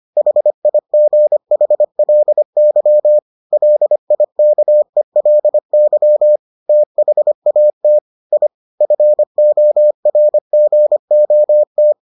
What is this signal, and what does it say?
Morse code, heard as a keyed tone, about 25 wpm, 600 Hz. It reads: HIGHLY LIKELY THAT I FORGOT